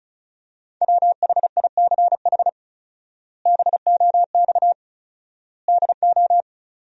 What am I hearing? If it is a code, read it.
Morse code, heard as a keyed tone, about 36 wpm, 700 Hz. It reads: WHICH BOX DO
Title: 4x Word-Spacing - Sentences from Top 500 Words - 35wpm